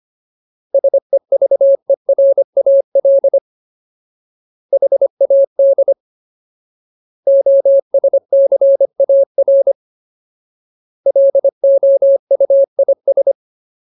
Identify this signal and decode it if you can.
Morse code, heard as a keyed tone, about 25 wpm, 550 Hz. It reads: SEVERAL HAD OSCAR LOUIS
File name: Ultimate Word List / 25WPM Ultimate